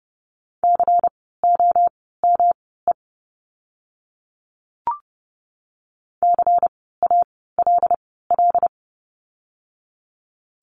Morse code, heard as a keyed tone, about 30 wpm, spaced as for 17 wpm, 700 Hz. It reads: COME CALL
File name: course / 30:10wpm / 115 Lesson 19f - Letter w - Words 30wpm